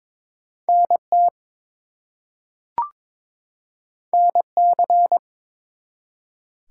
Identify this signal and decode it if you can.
Morse code, heard as a keyed tone, about 22 wpm, 700 Hz. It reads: NT NC
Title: US States and Canadian Provinces 22wpm